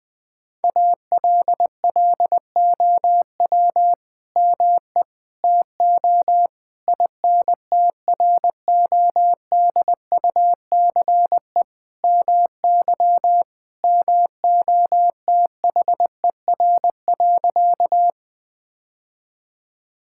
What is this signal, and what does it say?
Morse code, heard as a keyed tone, about 20 wpm, 700 Hz. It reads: ALLOW ME TO INTRODUCE MY MOTHER.
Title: Sentences from Top 1600 Words 20wpm